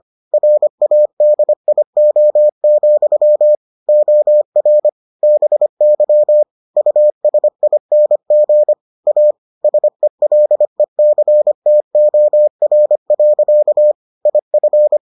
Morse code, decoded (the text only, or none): RADIO, OR BY USING A SELECTOR. IF